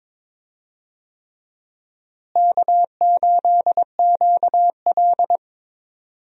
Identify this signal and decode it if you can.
Morse code, heard as a keyed tone, about 21 wpm, 700 Hz. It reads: K8QL
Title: USA Call Signs - 4 Characters - Repeated 3x - 22wpm